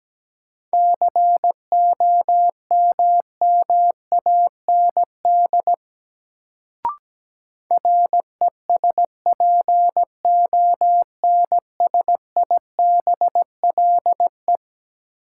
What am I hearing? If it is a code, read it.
Morse code, heard as a keyed tone, about 17 wpm, 700 Hz. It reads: COMMAND RESPONSIBLE
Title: Sentences from Top 1600 Words - New Word Drill Alt - 17wpm